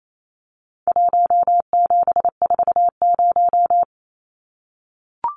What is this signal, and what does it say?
Morse code, heard as a keyed tone, about 28 wpm, 700 Hz. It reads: 1740
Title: Numbers - 4-Digit 28wpm